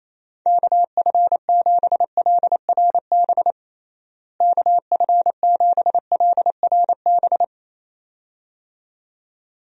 Morse code, heard as a keyed tone, about 28 wpm, 700 Hz. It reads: KF7LRB KF7LRB
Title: USA Call Signs - 6 Characters - Repeated 2x - 28wpm